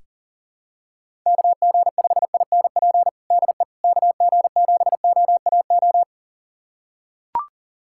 Morse code, heard as a keyed tone, about 40 wpm, 700 Hz. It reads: KG5INP DE KG7OAO
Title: USA Callsign DE Callsign - 6 Characters - 40wpm